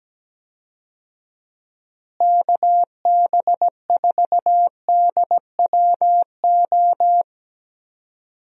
Morse code, heard as a keyed tone, about 17 wpm, 700 Hz. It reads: KB4DWO
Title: USA Call Signs - 6 Characters - Repeated 2x - 17wpm